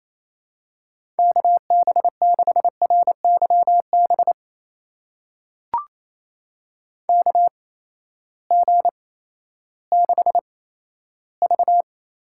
Morse code, heard as a keyed tone, about 28 wpm, spaced as for 14 wpm, 700 Hz. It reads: KB6RYB KG6V
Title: USA Call Signs - 6 Characters - 8x Character Spacing - 28wpm